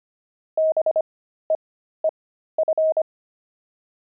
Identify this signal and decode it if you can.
Morse code, heard as a keyed tone, about 25 wpm, spaced as for 13 wpm, 630 Hz. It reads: BEEF
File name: pangram-fs-13-cs-25-monks